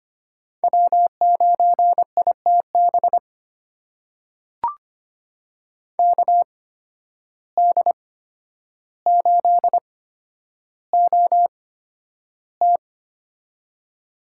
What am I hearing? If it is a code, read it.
Morse code, heard as a keyed tone, about 25 wpm, spaced as for 11 wpm, 700 Hz. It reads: W9ITB KD8OT
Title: USA Call Signs - 5 Characters - 8x Character Spacing - 25wpm